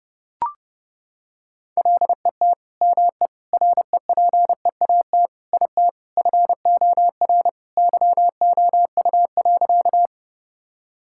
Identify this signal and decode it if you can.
Morse code, heard as a keyed tone, about 32 wpm, 700 Hz. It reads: LET ME REPEAT IT FOR YOU.
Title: Sentences from Top 1600 Words 30wpm